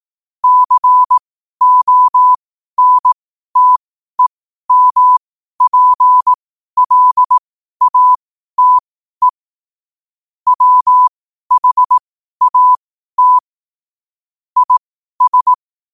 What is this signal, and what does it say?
Morse code, heard as a keyed tone, about 18 wpm, spaced as for 12 wpm, 1000 Hz. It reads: CONTEMPLATE WHAT IS
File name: qotdfiles / 12wpm